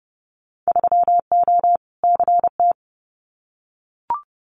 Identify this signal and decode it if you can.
Morse code, heard as a keyed tone, about 30 wpm, 700 Hz. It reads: 3O CT
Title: Winter Field Day Exchanges Encore 30wpm